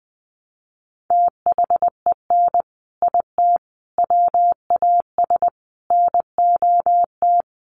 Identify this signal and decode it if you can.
Morse code, heard as a keyed tone, about 20 wpm, 700 Hz. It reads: THEN IT WAS NOT
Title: Sentences from Top 1300 Words 20wpm